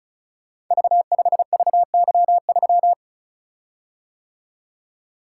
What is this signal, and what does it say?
Morse code, heard as a keyed tone, about 36 wpm, 700 Hz. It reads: V5VY3
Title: Code Groups - 5 Letter:Numbers - 35wpm